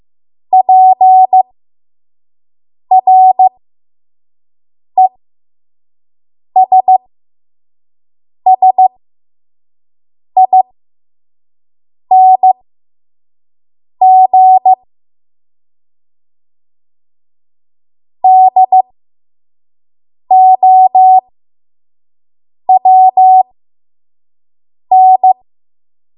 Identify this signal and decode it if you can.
Morse code, heard as a keyed tone, about 15 wpm, spaced as for 5 wpm, 750 Hz. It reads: PRESSING DOWN
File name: Morse / 05WPM